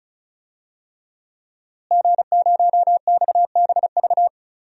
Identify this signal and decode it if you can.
Morse code, heard as a keyed tone, about 36 wpm, 700 Hz. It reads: G0XBV
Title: DXCC-100 Call Signs - 5 Characters - Repeated 3x - 35wpm